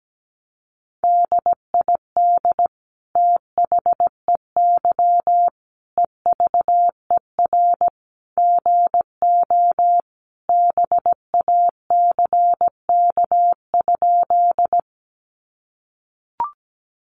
Morse code, heard as a keyed tone, about 17 wpm, 700 Hz. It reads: DID THEY EVER GO BACK?
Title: Sentences from Top 400 Words 17wpm